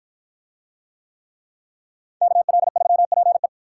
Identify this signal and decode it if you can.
Morse code, heard as a keyed tone, about 53 wpm, 700 Hz. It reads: KL3PE